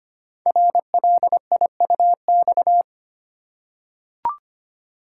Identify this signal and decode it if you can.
Morse code, heard as a keyed tone, about 25 wpm, 690 Hz. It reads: RLIUX
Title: Code Groups - 5 Letter:Numbers - 25wpm